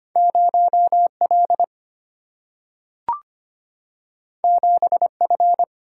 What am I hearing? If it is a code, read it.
Morse code, heard as a keyed tone, about 25 wpm, 700 Hz. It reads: 0 L 7 F
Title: Single-Letter Single-Number 25wpm